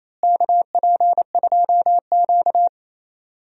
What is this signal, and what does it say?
Morse code, heard as a keyed tone, about 28 wpm, 700 Hz. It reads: KP2Q
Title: DXCC-100 Call Signs - 4 Characters - Repeated 2x - 28wpm